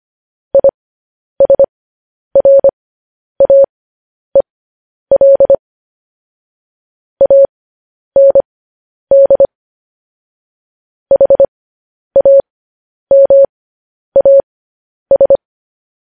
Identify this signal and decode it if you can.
Morse code, heard as a keyed tone, about 25 wpm, spaced as for 10 wpm, 550 Hz. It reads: ISRAEL AND HAMAS
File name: podcast / News headlines-1x-25x10